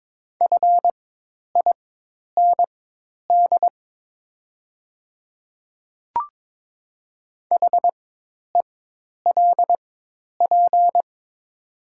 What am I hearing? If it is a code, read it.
Morse code, heard as a keyed tone, about 22 wpm, spaced as for 10 wpm, 700 Hz. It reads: FIND HELP
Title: 4x Character Spacing - Top 200 Words - 22wpm